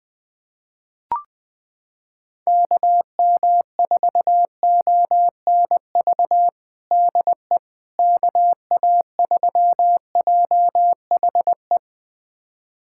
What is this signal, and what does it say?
Morse code, heard as a keyed tone, about 20 wpm, 700 Hz. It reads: KM4ONV DE KA3JHE
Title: USA Callsign DE Callsign - 6 Characters - 20wpm